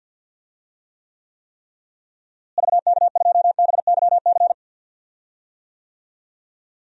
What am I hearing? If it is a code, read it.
Morse code, heard as a keyed tone, about 50 wpm, 700 Hz. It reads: VK2BYC
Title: DXCC-100 Call Signs - 6 Characters - Repeated 3x - 50wpm